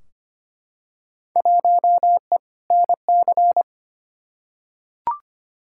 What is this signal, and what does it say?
Morse code, heard as a keyed tone, about 25 wpm, 700 Hz. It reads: 1E NC
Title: Field Day Exchanges - 25wpm